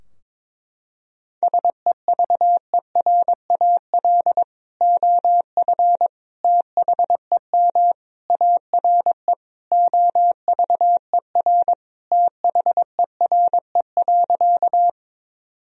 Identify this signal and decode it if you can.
Morse code, heard as a keyed tone, about 22 wpm, 700 Hz. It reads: SEVERAL OF THEM ARE OVER THERE.
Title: Sentences from Top 1000 Words - Review Part 2 - 22wpm